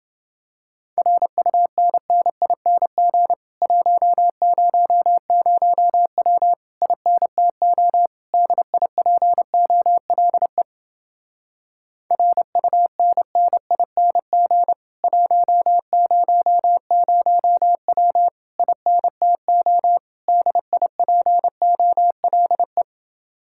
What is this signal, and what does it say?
Morse code, heard as a keyed tone, about 28 wpm, 700 Hz. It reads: RUNNING 100W INTO DIPOLE RUNNING 100W INTO DIPOLE